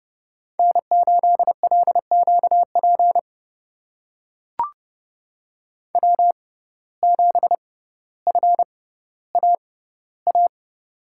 Morse code, heard as a keyed tone, about 32 wpm, spaced as for 16 wpm, 700 Hz. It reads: N8LQP W7FAA
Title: USA Call Signs - 5 Characters - 6x Character Spacing - 30wpm